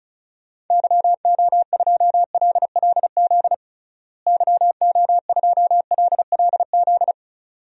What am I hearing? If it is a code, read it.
Morse code, heard as a keyed tone, about 34 wpm, 700 Hz. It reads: YO2LLZ YO2LLZ
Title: DXCC-100 Call Signs - 6 Characters - Repeated 2x - 35wpm